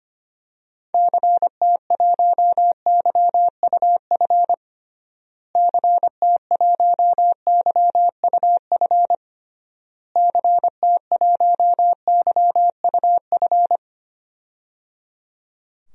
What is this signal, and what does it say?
Morse code, heard as a keyed tone, about 25 wpm, 700 Hz. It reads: CT1YUF CT1YUF CT1YUF
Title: DXCC-100 Call Signs - 6 Characters - Repeated 3x - 25wpm